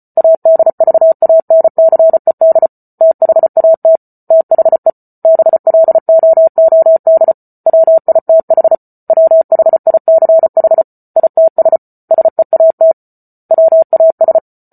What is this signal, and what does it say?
Morse code, heard as a keyed tone, about 34 wpm, 650 Hz. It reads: ADVANCED THAT THE BLOOD WITH WHICH ITS SEAT WAS